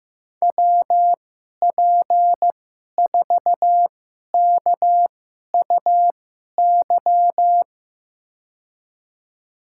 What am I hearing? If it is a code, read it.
Morse code, heard as a keyed tone, about 15 wpm, spaced as for 11 wpm, 700 Hz. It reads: WP4KUY
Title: USA Call Signs - 6 Characters - 2x Character Spacing - 15wpm